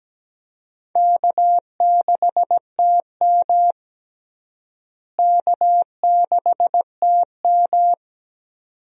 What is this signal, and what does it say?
Morse code, heard as a keyed tone, about 17 wpm, 700 Hz. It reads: K6TM K6TM
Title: USA Call Signs - 4 Characters - Repeated 2x - 17wpm